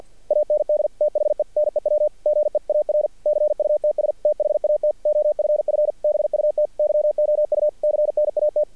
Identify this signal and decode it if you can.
Morse code, heard as a keyed tone, about 54 wpm, 600 Hz. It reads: AND THE NEW GEAR CUTS THAT OFF BUT YOU CNAT
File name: archived / CRSnet